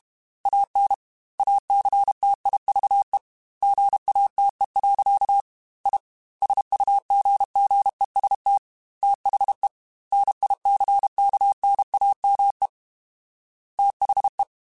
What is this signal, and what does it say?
Morse code, heard as a keyed tone, about 32 wpm, 800 Hz. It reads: AN ACTIVE GATE. I SUGGEST THE NICKNAME THE
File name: Technical Subjects / Technical Subjects-mp3s / AND Gate @30